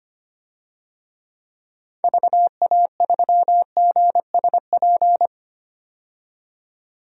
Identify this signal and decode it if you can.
Morse code, heard as a keyed tone, about 25 wpm, 700 Hz. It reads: VA3GSP
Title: DXCC-100 Call Signs - 6 Characters - 25wpm